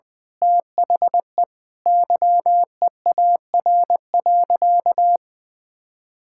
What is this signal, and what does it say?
Morse code, heard as a keyed tone, about 20 wpm, 700 Hz. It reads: THE YEAR.
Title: Sentences with 8-Word Phrases - 20wpm